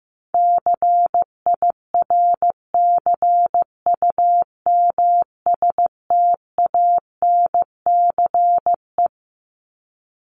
Morse code, heard as a keyed tone, about 15 wpm, 700 Hz. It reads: CIRCUMSTANCE
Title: ES:S Suffix - Top 100 Words with Base Words 15wpm